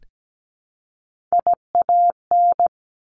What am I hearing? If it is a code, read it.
Morse code, heard as a keyed tone, about 17 wpm, 700 Hz. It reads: IAN
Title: Top Three Letter Combinations in Words 17wpm